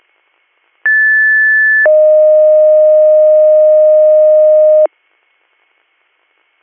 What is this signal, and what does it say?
Two-tone page, heard as a one-second tone, then a three-second tone, about 1750 then 620 Hz. It is heard from 0.8 to 4.9 s.
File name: 200- .0-call 6932